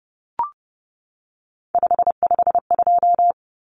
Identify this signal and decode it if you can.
Morse code, heard as a keyed tone, about 30 wpm, 700 Hz. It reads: 552
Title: Numbers - 3-Digit 30wpm